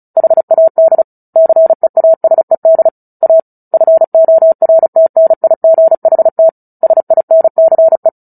Morse code, heard as a keyed tone, about 34 wpm, 650 Hz. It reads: HAD CEASED A FORTNIGHT SINCE
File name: Book Two / 09-Chapter 9 - 35 WPM